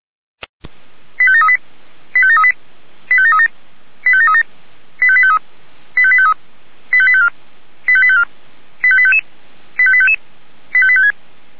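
Five-tone selective call (ZVEI), heard as quick runs of five tones, tones of about 70 ms. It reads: 85738 85738 85737 85737 85853 85853 85854 85854 85855 85855 85856
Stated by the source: ZVEI five-tone selective call received by SDR.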